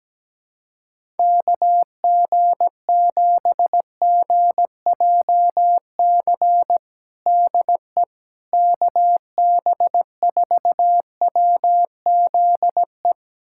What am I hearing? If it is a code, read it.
Morse code, heard as a keyed tone, about 17 wpm, 700 Hz. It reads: KG7GJC DE KB4WZE
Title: USA Callsign DE Callsign 17wpm